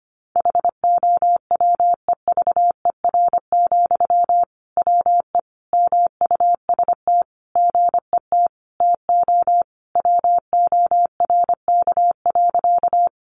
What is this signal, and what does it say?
Morse code, heard as a keyed tone, about 25 wpm, 700 Hz. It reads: HOWEVER, WE MUST GET TO WORK.
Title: Sentences from Top 600 Words 25wpm